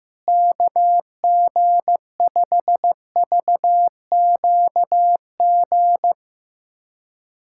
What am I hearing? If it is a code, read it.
Morse code, heard as a keyed tone, about 15 wpm, 700 Hz. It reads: KG5VQG
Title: USA Callsign DE Callsign 15wpm